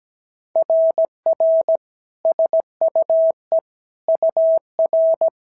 Morse code, heard as a keyed tone, about 17 wpm, 650 Hz. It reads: RR SUE UR
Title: SOTA QSO Practice 17wpm